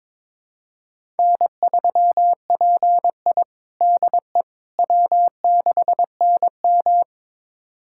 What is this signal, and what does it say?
Morse code, heard as a keyed tone, about 22 wpm, 700 Hz. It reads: N3PI DE W6NM
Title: USA Callsign DE Callsign - 4 Characters - 22wpm